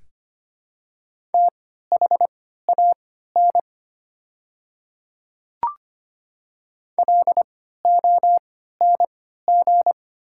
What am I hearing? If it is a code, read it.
Morse code, heard as a keyed tone, about 25 wpm, spaced as for 14 wpm, 700 Hz. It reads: THAN LONG
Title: 3x Character Spacing - Top 100 Words - 25wpm